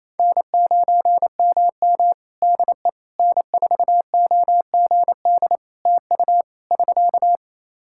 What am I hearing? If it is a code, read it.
Morse code, heard as a keyed tone, about 28 wpm, 700 Hz. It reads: N9MM DE N4OGD TU <SK>
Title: K1USN SST QSO Practice 28wpm